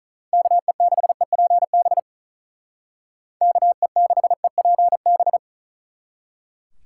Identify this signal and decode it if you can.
Morse code, heard as a keyed tone, about 46 wpm, 700 Hz. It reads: KE6EPB KE6EPB